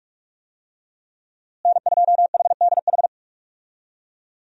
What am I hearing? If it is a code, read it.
Morse code, heard as a keyed tone, about 42 wpm, 700 Hz. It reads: N2HDH
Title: USA Call Signs - 5 Characters - Repeated 2x - 45wpm